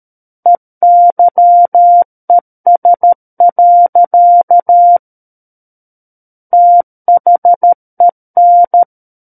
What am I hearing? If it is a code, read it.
Morse code, heard as a keyed tone, about 13 wpm, 700 Hz. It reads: EYES. THEN
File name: Book One / 04-Chapter 4 - 13 WPM